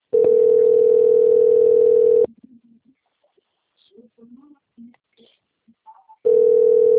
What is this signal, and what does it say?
Ringback tone (North America), 440 + 480 Hz, with speech.
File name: out- - - - - .9588